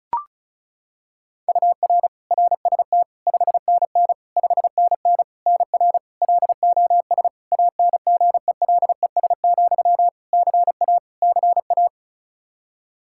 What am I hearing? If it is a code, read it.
Morse code, heard as a keyed tone, about 34 wpm, 700 Hz. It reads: UR RST 5NN 5NN NR LOS ANGELES, CA CA
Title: K1USN SST QSO Practice 35wpm